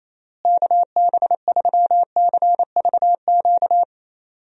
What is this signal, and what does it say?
Morse code, heard as a keyed tone, about 28 wpm, 700 Hz. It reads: KB3CVQ
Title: USA Call Signs - 6 Characters - Repeated 3x - 28wpm